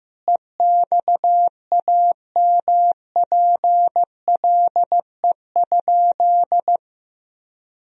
Morse code, heard as a keyed tone, about 15 wpm, 700 Hz. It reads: EXAMPLE?
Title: Sentences from Top 1000 Words - Review Part 1 - 15wpm